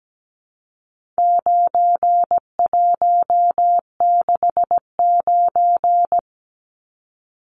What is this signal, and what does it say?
Morse code, heard as a keyed tone, about 17 wpm, 700 Hz. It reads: 9169
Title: Numbers - 4-Digit 17wpm